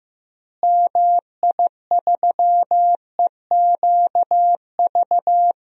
Morse code, heard as a keyed tone, about 15 wpm, 700 Hz. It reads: MI3EQV
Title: DXCC-100 Call Signs - 6 Characters - Repeated 3x - 15wpm